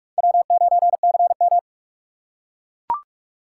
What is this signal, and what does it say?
Morse code, heard as a keyed tone, about 44 wpm, 700 Hz. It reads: W9CM